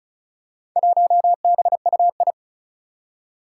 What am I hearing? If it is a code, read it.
Morse code, heard as a keyed tone, about 36 wpm, 700 Hz. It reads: 1BUI
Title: Four Challenging Letter-Number Combinations 35wpm